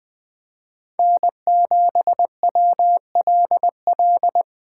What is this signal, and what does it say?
Morse code, heard as a keyed tone, about 21 wpm, 700 Hz. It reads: N7WLL